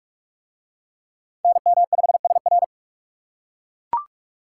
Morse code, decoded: NM5SR

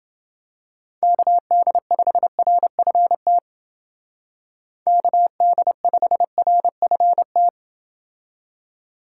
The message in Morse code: KD5RFT KD5RFT

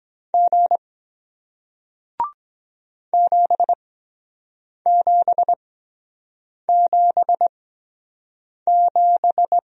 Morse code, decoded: G 7777

